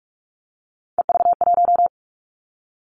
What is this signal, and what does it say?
Morse code, heard as a keyed tone, about 46 wpm, 700 Hz. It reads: EV1
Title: Three Challenging Letter-Number Combinations 45wpm